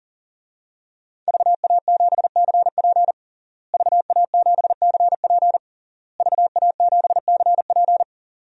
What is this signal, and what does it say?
Morse code, heard as a keyed tone, about 37 wpm, 700 Hz. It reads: VA7CP VA7CP VA7CP
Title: DXCC-100 Call Signs - 5 Characters - Repeated 3x - 40wpm